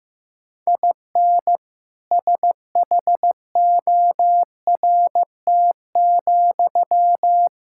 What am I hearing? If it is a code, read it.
Morse code, heard as a keyed tone, about 15 wpm, 700 Hz. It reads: IN SHORT,
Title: Sentences from Top 1000 Words - Review Part 5 - 15wpm